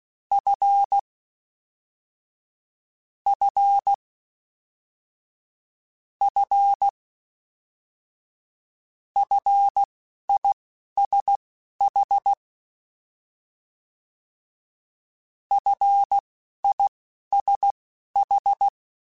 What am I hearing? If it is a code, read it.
Morse code, heard as a keyed tone, about 16 wpm, spaced as for 11 wpm, 800 Hz. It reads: F F F FISH FISH
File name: F and L / F and L mp3 / F and L Prefixes 3x2 Twice @10